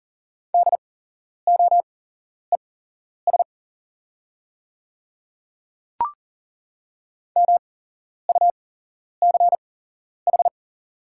Morse code, heard as a keyed tone, about 40 wpm, spaced as for 11 wpm, 700 Hz. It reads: DOES MUCH
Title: 8x Character Spacing - Top 200 Words - 40wpm